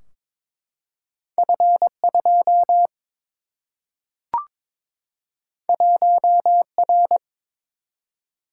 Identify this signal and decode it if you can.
Morse code, heard as a keyed tone, about 22 wpm, 700 Hz. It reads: F 2 1 R
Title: Single-Letter Single-Number 22wpm